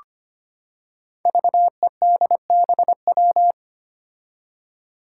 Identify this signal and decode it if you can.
Morse code, heard as a keyed tone, about 25 wpm, 700 Hz. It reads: VEDBW
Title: Five Challenging Letter-Number Combinations 25wpm